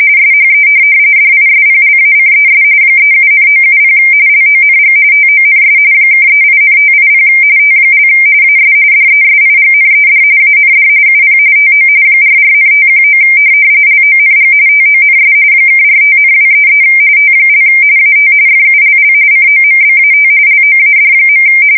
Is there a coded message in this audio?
Yes: RTTY from 0.2 to 21.4 s, about 45.45 baud. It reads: SHE DISPLACES 37,970 LONG TONS (38,580 T) AS DESIGNED AND UP TO 44,519 LONG TONS (45,233 T) AT FULL COMBAT LOAD.